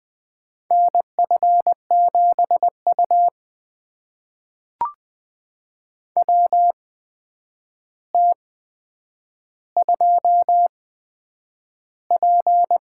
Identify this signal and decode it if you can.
Morse code, heard as a keyed tone, about 20 wpm, spaced as for 9 wpm, 700 Hz. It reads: NF7U WT2P
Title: USA Call Signs - 4 Characters - 8x Character Spacing - 20wpm